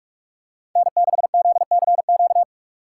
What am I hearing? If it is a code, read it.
Morse code, heard as a keyed tone, about 41 wpm, 700 Hz. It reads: N6ZCQ